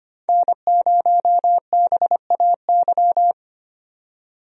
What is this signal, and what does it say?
Morse code, heard as a keyed tone, about 25 wpm, 690 Hz. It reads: N0BAY
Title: USA Call Signs - 5 Characters - 25wpm